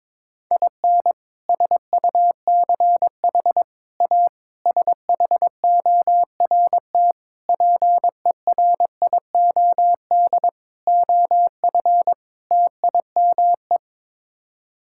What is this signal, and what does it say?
Morse code, heard as a keyed tone, about 22 wpm, 700 Hz. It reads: IN SUCH A SHORT PERIOD OF TIME
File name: practice / 22wpm / Sets of 7 Words - 22wpm